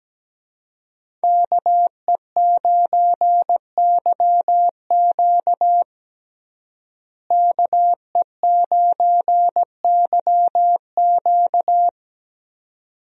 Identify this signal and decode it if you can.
Morse code, heard as a keyed tone, about 17 wpm, 700 Hz. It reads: KE9YQ KE9YQ
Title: USA Call Signs - 5 Characters - Repeated 2x - 17wpm